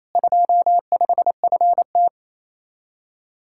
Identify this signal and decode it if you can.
Morse code, heard as a keyed tone, about 34 wpm, 700 Hz. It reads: 25FT
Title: Speed-Racing Common QSO Phrases - 28wpm